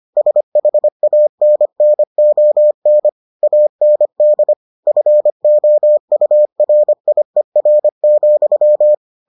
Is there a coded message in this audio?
Yes: Morse code from 0.2 to 8.9 s, heard as a keyed tone, about 25 wpm, 600 Hz. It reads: SHANNON AND FOURIER,